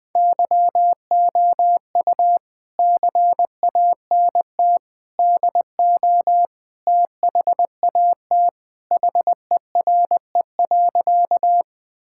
Morse code, heard as a keyed tone, about 21 wpm, 700 Hz. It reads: YOU CANT DO THAT HERE.
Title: Sentences with 5-Word Phrases - 20wpm